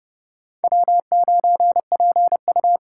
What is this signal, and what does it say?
Morse code, heard as a keyed tone, about 29 wpm, 700 Hz. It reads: W9PU